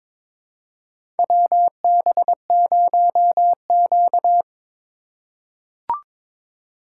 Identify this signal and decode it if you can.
Morse code, heard as a keyed tone, about 22 wpm, 700 Hz. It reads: WB0Q
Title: USA Call Signs - 4 Characters - 4x Character Spacing - 22wpm